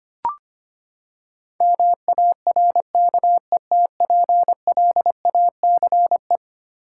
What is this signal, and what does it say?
Morse code, heard as a keyed tone, about 25 wpm, 700 Hz. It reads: MARKETPLACE
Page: morse-code-ninja-practice-25wpm